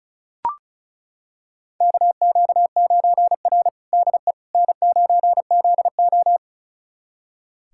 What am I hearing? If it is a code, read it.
Morse code, heard as a keyed tone, about 36 wpm, 700 Hz. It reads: KQ9R DE N9ZO
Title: USA Callsign DE Callsign - 4 Characters - 35wpm